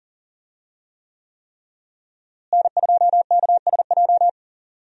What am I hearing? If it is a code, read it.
Morse code, heard as a keyed tone, about 40 wpm, 700 Hz. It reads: N2KSJ
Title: USA Call Signs - 5 Characters - Repeated 3x - 40wpm